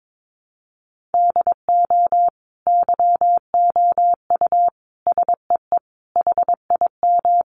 Morse code, heard as a keyed tone, about 22 wpm, 700 Hz. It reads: DO YOU SEE HIM
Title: Sentences from Top 1400 Words 22wpm